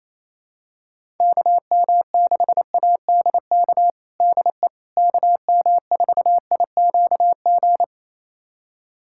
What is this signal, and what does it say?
Morse code, heard as a keyed tone, about 28 wpm, 700 Hz. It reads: KM6ADK DE KM4IQG
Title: USA Callsign DE Callsign - 6 Characters - 28wpm